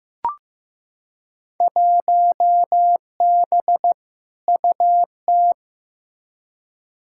Morse code, heard as a keyed tone, about 15 wpm, 700 Hz. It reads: 1B UT
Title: Field Day Exchanges - 15wpm